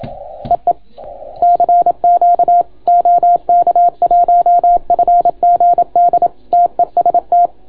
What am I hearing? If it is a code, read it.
Morse code, heard as a keyed tone, about 27 wpm, 690 Hz. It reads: EE CQ OK1FGD TEST